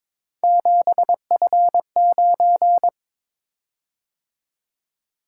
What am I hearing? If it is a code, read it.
Morse code, heard as a keyed tone, about 22 wpm, 700 Hz. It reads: 7F9
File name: practice / 22wpm / Three Letter-Number Combinations - 22wpm